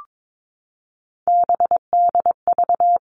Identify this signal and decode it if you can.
Morse code, heard as a keyed tone, about 22 wpm, 700 Hz. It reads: BDV
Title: All Cut Numbers - 3-Digit 22wpm